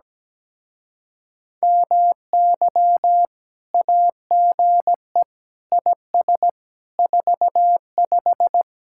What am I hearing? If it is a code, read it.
Morse code, heard as a keyed tone, about 17 wpm, 700 Hz. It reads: MY AGE IS 45
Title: Speed-Racing Common QSO Phrases - 17wpm